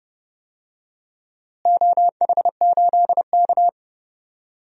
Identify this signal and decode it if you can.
Morse code, heard as a keyed tone, about 30 wpm, 700 Hz. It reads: OH8K
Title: DXCC-100 Call Signs - 4 Characters - Repeated 2x - 30wpm